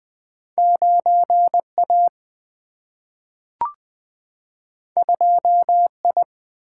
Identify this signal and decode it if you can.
Morse code, heard as a keyed tone, about 20 wpm, 700 Hz. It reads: 9 A 2 I
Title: Single-Letter Single-Number 20wpm